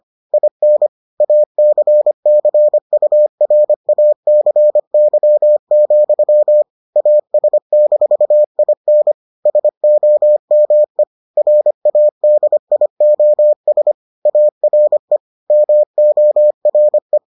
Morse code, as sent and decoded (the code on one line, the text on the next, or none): .. -. / .- -.-. -.-. ..- .-. .- -.-. -.-- --..-- / .- ... -....- .. -. / ... --- -- . / .-. .- -.. .. --- ... / .- .-. / -- --- .-. .
IN ACCURACY, AS-IN SOME RADIOS ARE MORE